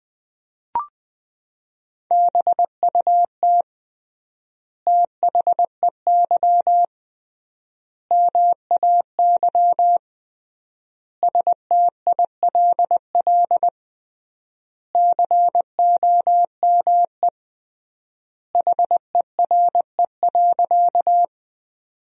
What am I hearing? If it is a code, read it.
Morse code, heard as a keyed tone, about 20 wpm, 700 Hz. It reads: BUT THEY MAY STILL COME HERE.
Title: 3x Word-Spacing - Sentences from Top 300 Words - 20wpm